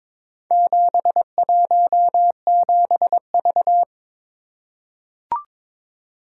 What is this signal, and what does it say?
Morse code, heard as a keyed tone, about 22 wpm, 700 Hz. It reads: 717V
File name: practice / 22wpm / Four Letter-Number Combinations - 22wpm